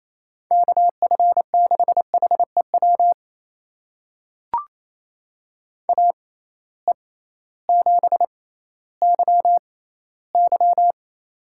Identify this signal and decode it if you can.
Morse code, heard as a keyed tone, about 28 wpm, spaced as for 15 wpm, 700 Hz. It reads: KF6HEW AE7YY
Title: USA Call Signs - 6x Character Spacing - 28wpm